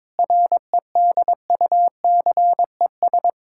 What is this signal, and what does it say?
Morse code, heard as a keyed tone, about 22 wpm, 700 Hz. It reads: REDUCES
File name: Difficult Sentences with Hints - 22wpm